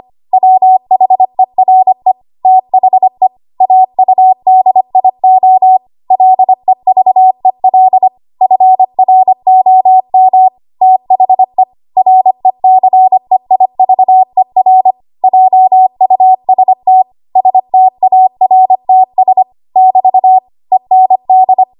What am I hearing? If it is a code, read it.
Morse code, heard as a keyed tone, about 25 wpm, 750 Hz. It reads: WHERE THE AUDIO LEVEL FROM THE RECEIVER JUST STARTS <BT> END
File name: Archive / 25 WPM / 25WPM